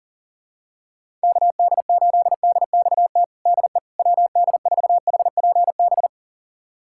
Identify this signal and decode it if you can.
Morse code, heard as a keyed tone, about 37 wpm, 700 Hz. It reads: KD8DXT DE WD4HPB
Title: USA Callsign DE Callsign - 6 Characters - 40wpm